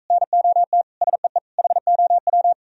Morse code, heard as a keyed tone, about 45 wpm, 700 Hz. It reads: NOT SEE HOW